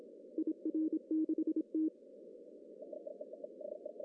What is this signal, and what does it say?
Morse code, heard as a keyed tone, about 27 wpm, 320 Hz. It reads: IR6T
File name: CW / cqww-cw-k5tr